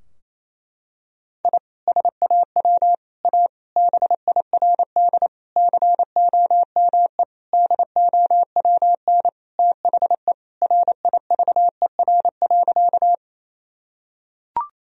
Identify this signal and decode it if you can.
Morse code, heard as a keyed tone, about 27 wpm, 700 Hz. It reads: I SAW A BIRD COME DOWN THE RIVER.
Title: Sentences from Top 1000 Words - Review Part 2 - 28wpm